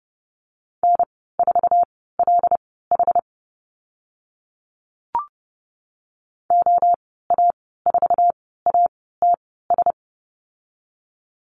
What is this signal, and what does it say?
Morse code, heard as a keyed tone, about 30 wpm, spaced as for 17 wpm, 700 Hz. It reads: N4LH OA4ATS
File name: Lesson 14g - Number 2 - Call Signs 30wpm